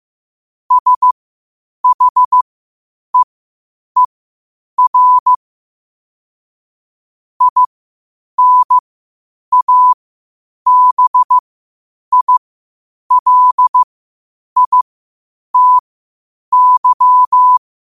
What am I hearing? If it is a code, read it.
Morse code, heard as a keyed tone, about 15 wpm, spaced as for 8 wpm, 1000 Hz. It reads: SHEER INABILITY
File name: qotdfiles / 8wpm